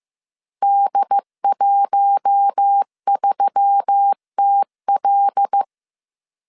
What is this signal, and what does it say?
Morse code, heard as a keyed tone, about 15 wpm, 800 Hz. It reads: D13TL
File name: CW / DET / 1 6 1500C S B 14wpm 2958